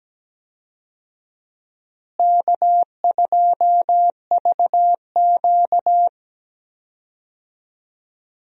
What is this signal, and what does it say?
Morse code, heard as a keyed tone, about 17 wpm, 700 Hz. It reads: K2VQ
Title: USA Call Signs - 4 Characters - Repeated 2x - 17wpm